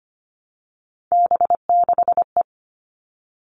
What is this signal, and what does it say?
Morse code, heard as a keyed tone, about 25 wpm, 700 Hz. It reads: B6E
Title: All Cut Numbers - 3-Digit 25wpm